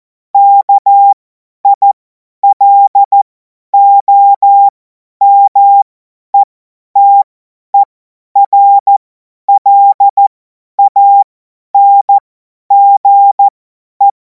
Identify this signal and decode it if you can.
Morse code, heard as a keyed tone, about 14 wpm, spaced as for 10 wpm, 800 Hz. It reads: KILOMETERLANGE